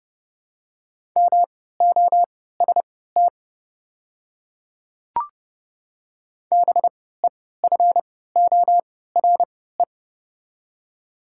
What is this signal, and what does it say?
Morse code, heard as a keyed tone, about 32 wpm, spaced as for 17 wpm, 690 Hz. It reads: MOST BEFORE
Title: 3x Character Spacing - Top 200 Words - 30wpm